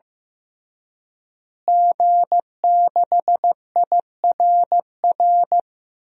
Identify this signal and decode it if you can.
Morse code, heard as a keyed tone, about 15 wpm, 700 Hz. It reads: G6IRR
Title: DXCC-100 Call Signs - 5 Characters - Repeated 3x - 15wpm